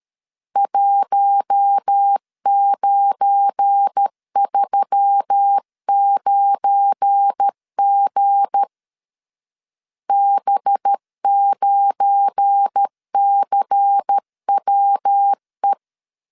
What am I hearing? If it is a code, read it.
Morse code, heard as a keyed tone, about 13 wpm, 800 Hz. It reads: 1939G B9CWE
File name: LEZIONI / 7 GF9Z / 7 GF9 12wpm 2319